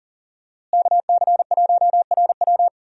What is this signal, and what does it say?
Morse code, heard as a keyed tone, about 40 wpm, 700 Hz. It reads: KC1RW
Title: USA Call Signs - 5 Characters - Repeated 2x - 40wpm